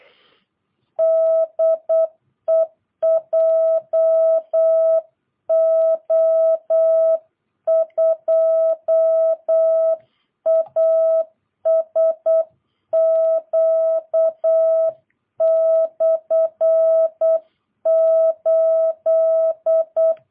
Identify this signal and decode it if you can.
Morse code, heard as a keyed tone, about 8 wpm, 650 Hz. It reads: DE JO2ASQ/8
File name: jh8cbh / 2022cw